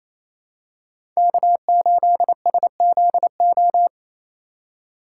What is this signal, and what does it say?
Morse code, heard as a keyed tone, about 28 wpm, 700 Hz. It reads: K8SZO
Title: USA Call Signs - 5 Characters - 2x Character Spacing - 28wpm